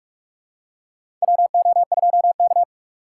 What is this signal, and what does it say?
Morse code, heard as a keyed tone, about 48 wpm, 700 Hz. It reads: WO2K
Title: USA Call Signs - 4 Characters - 6x Character Spacing - 45wpm